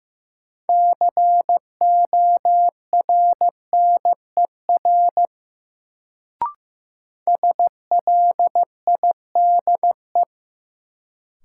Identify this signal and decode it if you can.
Morse code, heard as a keyed tone, about 15 wpm, 700 Hz. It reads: CORNER SLIDE
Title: Sentences from Top 1600 Words - New Word Drill - 15wpm